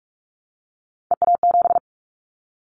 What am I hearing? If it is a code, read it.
Morse code, heard as a keyed tone, about 44 wpm, 700 Hz. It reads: EA7
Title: Callsign Prefixes for Top 25 DXCC - Repeated 3x - 45wpm